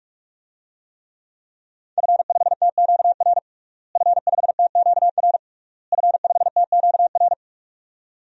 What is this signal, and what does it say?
Morse code, heard as a keyed tone, about 41 wpm, 700 Hz. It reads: F5TQR F5TQR F5TQR
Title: DXCC-100 Call Signs - 5 Characters - Repeated 3x - 45wpm